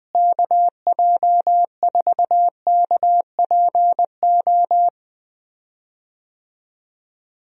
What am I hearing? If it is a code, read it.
Morse code, heard as a keyed tone, about 20 wpm, 700 Hz. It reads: KJ4KPO
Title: USA Call Signs - 6 Characters - Repeated 2x - 20wpm